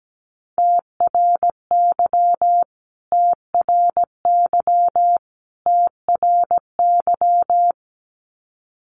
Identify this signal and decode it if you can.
Morse code, heard as a keyed tone, about 17 wpm, 700 Hz. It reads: TRY TRY TRY
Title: Top 500 Words Repeated 17wpm